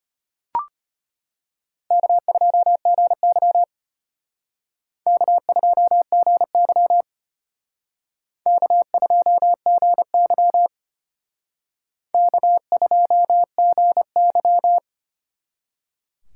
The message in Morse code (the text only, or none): K2GY K2GY K2GY K2GY